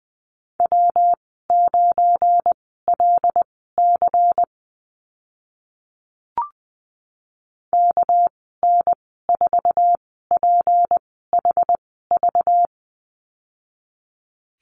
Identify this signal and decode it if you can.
Morse code, heard as a keyed tone, about 20 wpm, spaced as for 14 wpm, 700 Hz. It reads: W9LC KN4PHV
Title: 238 Lesson 38g - Letter q - Call Signs 20wpm